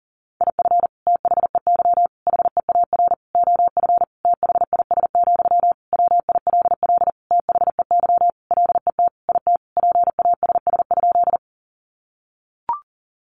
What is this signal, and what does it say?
Morse code, heard as a keyed tone, about 40 wpm, 700 Hz. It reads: IF THEY HEAR OF THIS, WILL THEY LET IT PASS?
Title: Sentences from Top 400 Words 40wpm